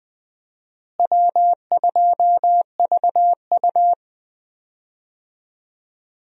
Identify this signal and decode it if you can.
Morse code, heard as a keyed tone, about 20 wpm, 700 Hz. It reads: W2VU
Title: USA Call Signs - 4 Characters - Repeated 2x - 20wpm